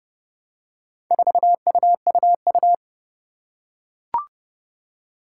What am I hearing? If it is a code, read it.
Morse code, heard as a keyed tone, about 30 wpm, 700 Hz. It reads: 4UUU
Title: Four Challenging Letter-Number Combinations 30wpm